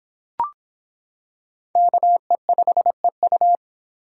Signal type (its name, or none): Morse code